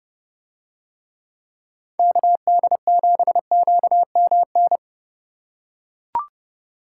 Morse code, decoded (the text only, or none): KD7QMN